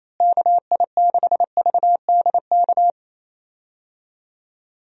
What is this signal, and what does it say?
Morse code, heard as a keyed tone, about 27 wpm, 700 Hz. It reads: KI6VDK
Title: USA Call Signs - 6 Characters - Repeated 2x - 28wpm